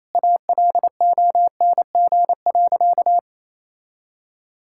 Morse code, heard as a keyed tone, about 28 wpm, 700 Hz. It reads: ALONG.
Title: Sentences from Top 1600 Words 28wpm